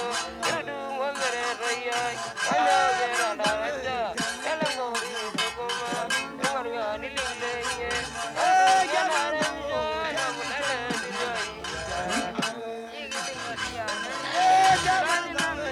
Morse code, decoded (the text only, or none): none